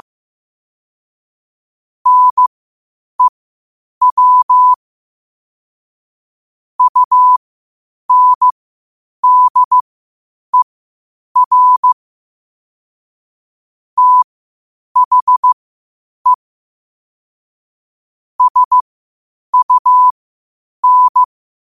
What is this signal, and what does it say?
Morse code, heard as a keyed tone, about 15 wpm, spaced as for 8 wpm, 1000 Hz. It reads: NEW UNDER THE SUN
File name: qotdfiles / 8wpm